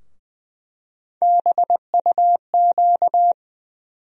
Morse code, heard as a keyed tone, about 20 wpm, 700 Hz. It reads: BUQ
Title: USA Call Sign Postfixes - 3 Letters 20wpm